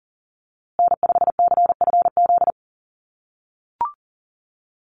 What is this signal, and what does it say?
Morse code, heard as a keyed tone, about 40 wpm, 700 Hz. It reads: N5CFZ
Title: USA Call Signs Repeated 2x 40wpm